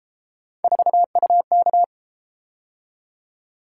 Morse code, heard as a keyed tone, about 31 wpm, 700 Hz. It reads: 4UK